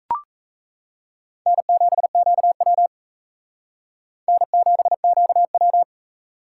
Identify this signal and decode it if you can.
Morse code, heard as a keyed tone, about 35 wpm, 700 Hz. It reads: N7QW N7QW